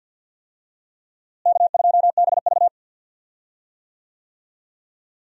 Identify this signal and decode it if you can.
Morse code, heard as a keyed tone, about 51 wpm, 700 Hz. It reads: K2BV
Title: USA Call Signs - 4 Characters - Repeated 2x - 50wpm